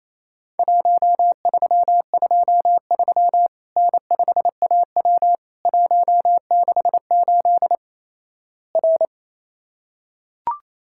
Morse code, heard as a keyed tone, about 28 wpm, 700 Hz. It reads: 1323 N5AW 168 R
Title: Worked All Europe DX Contest - QTC Exchange - 28wpm